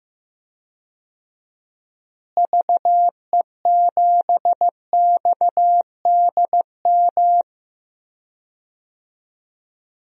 Morse code, heard as a keyed tone, about 15 wpm, 700 Hz. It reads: VE7XDM
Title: DXCC-100 Call Signs - 6 Characters - Repeated 3x - 15wpm